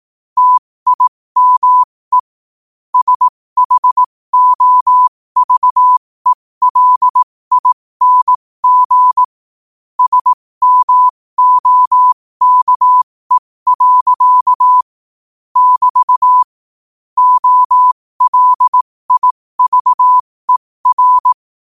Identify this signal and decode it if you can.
Morse code, heard as a keyed tone, about 18 wpm, spaced as for 15 wpm, 1000 Hz. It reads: TIME SHOVELING SMOKE. <BT> OLIVER
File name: qotdfiles / 15wpm